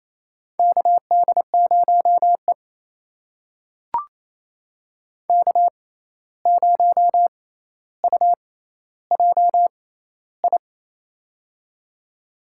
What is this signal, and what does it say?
Morse code, heard as a keyed tone, about 28 wpm, spaced as for 13 wpm, 700 Hz. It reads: KD0E K0UJI
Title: USA Call Signs - 6x Character Spacing - 28wpm